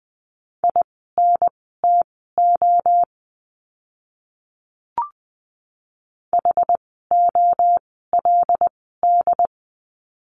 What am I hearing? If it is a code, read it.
Morse code, heard as a keyed tone, about 20 wpm, spaced as for 15 wpm, 700 Hz. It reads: INTO HOLD
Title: 101 Lesson 17f - Letter u - Words 20wpm